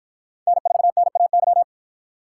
Morse code, heard as a keyed tone, about 52 wpm, 700 Hz. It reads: N4NAY